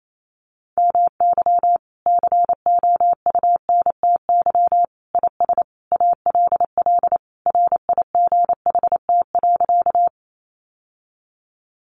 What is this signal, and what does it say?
Morse code, heard as a keyed tone, about 28 wpm, 700 Hz. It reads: MY COUNTY IS ALL RIGHT.
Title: Sentences from Top 1200 Words 28wpm